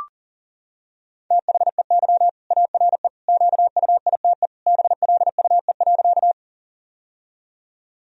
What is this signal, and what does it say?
Morse code, heard as a keyed tone, about 40 wpm, 700 Hz. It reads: THEY ARE QUITE BLUE.
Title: Sentences from Top 1000 Words - Review Part 3 - 40wpm